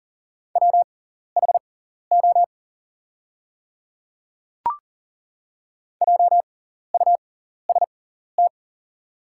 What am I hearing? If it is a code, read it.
Morse code, heard as a keyed tone, about 40 wpm, spaced as for 14 wpm, 690 Hz. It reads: WHO JUST
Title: 6x Character Spacing - Top 300 Words - 40wpm